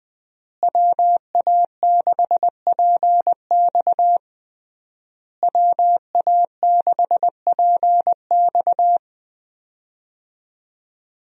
Morse code, heard as a keyed tone, about 21 wpm, 700 Hz. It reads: WA6PX WA6PX
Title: USA Call Signs - 5 Characters - Repeated 3x - 20wpm